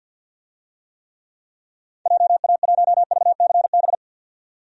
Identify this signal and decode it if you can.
Morse code, heard as a keyed tone, about 51 wpm, 700 Hz. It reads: JA1VCB
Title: DXCC-100 Call Signs - 6 Characters - 50wpm